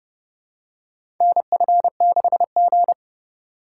Morse code, heard as a keyed tone, about 30 wpm, 700 Hz. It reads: NF6G
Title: USA Call Signs - 3x Character Spacing - 30wpm